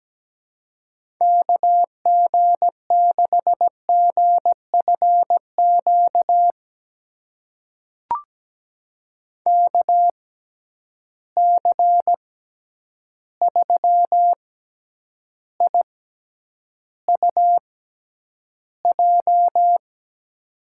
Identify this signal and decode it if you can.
Morse code, heard as a keyed tone, about 17 wpm, spaced as for 9 wpm, 700 Hz. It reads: KG6GFQ KC3IUJ
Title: USA Call Signs - 6 Characters - 6x Character Spacing - 17wpm